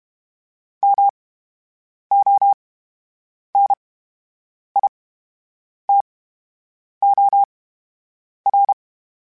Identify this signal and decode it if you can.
Morse code, heard as a keyed tone, about 32 wpm, 800 Hz. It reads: M O N I T O R
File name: Special INGs / Special INGs-mp3s / VerySlowToFaster Ing @30